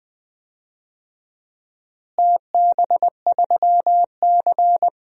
Morse code, heard as a keyed tone, about 20 wpm, 700 Hz. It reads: TB3C
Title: DXCC-100 Call Signs - 4 Characters - Repeated 2x - 20wpm